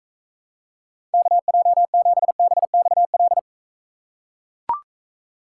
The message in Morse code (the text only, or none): KJ7BXL